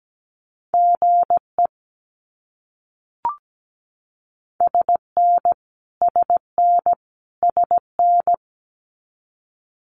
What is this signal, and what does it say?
Morse code, heard as a keyed tone, about 17 wpm, 700 Hz. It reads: GE SN SN SN
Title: Abbreviations Repeated 3x - 17wpm